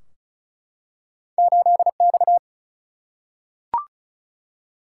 Morse code, heard as a keyed tone, about 36 wpm, 700 Hz. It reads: 8 X
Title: Single-Letter Single-Number 35wpm